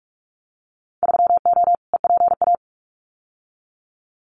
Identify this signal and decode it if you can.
Morse code, heard as a keyed tone, about 46 wpm, 700 Hz. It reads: 3O EPA
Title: Winter Field Day Exchanges Encore 45wpm